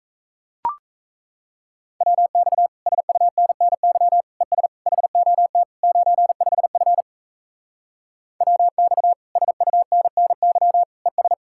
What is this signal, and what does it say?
Morse code, heard as a keyed tone, about 36 wpm, 700 Hz. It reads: WX SUNNY ES HOT 95F WX SUNNY ES